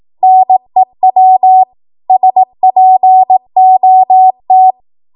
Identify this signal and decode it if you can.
Morse code, heard as a keyed tone, about 18 wpm, 750 Hz. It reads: NEW SPOT
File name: Archive / 18 WPM / 18WPM